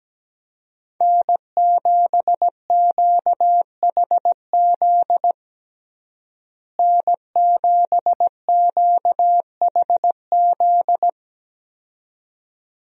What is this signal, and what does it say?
Morse code, heard as a keyed tone, about 17 wpm, 700 Hz. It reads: N7QHZ N7QHZ
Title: USA Call Signs - 5 Characters - Repeated 2x - 17wpm